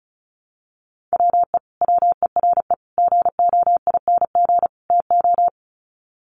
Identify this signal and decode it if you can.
Morse code, heard as a keyed tone, about 34 wpm, 700 Hz. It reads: WE WERE GOING TO
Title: Sets of 4 Words 35wpm